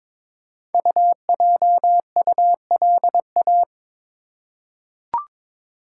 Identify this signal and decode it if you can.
Morse code, heard as a keyed tone, about 22 wpm, 700 Hz. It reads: UJULA
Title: Code Groups - 5 Letters - 22wpm